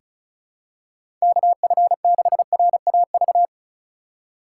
Morse code, heard as a keyed tone, about 36 wpm, 700 Hz. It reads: KF6RAV
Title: USA Call Signs - 3x Character Spacing - 35wpm